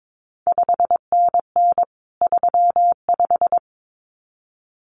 Morse code, heard as a keyed tone, about 22 wpm, 700 Hz. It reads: 5NN 35
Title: CQ WW DX Contest Exchange Encore - 22wpm